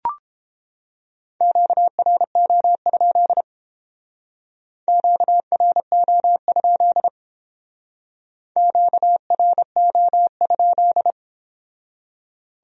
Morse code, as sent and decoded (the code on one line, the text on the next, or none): --.- .-. --- ..--.. / --.- .-. --- ..--.. / --.- .-. --- ..--..
QRO? QRO? QRO?